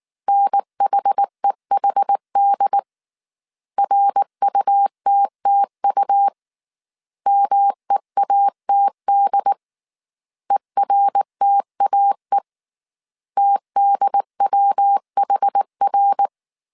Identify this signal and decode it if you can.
Morse code, heard as a keyed tone, about 19 wpm, 800 Hz. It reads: NHEHD RUTTU MEATD ERTAE TDWHR